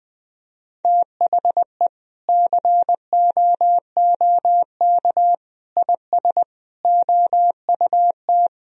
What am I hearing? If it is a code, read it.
Morse code, heard as a keyed tone, about 20 wpm, 700 Hz. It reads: THE COOK IS OUT
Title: Sentences from Top 1000 Words - Review Part 4 - 20wpm